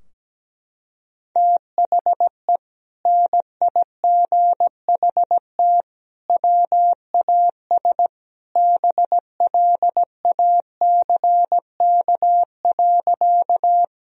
Morse code, heard as a keyed tone, about 17 wpm, 700 Hz. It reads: THE NIGHT WAS BLACK.
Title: Sentences from Top 1000 Words - Review Part 2 - 17wpm